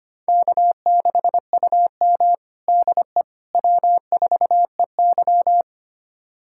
Morse code, heard as a keyed tone, about 25 wpm, 700 Hz. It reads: K6UM DE W4EY
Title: USA Callsign DE Callsign - 4 Characters - 25wpm